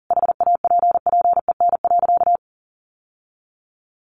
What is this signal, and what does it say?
Morse code, heard as a keyed tone, about 40 wpm, 700 Hz. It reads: HAPPEN.
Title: Sentences from Top 1200 Words 40wpm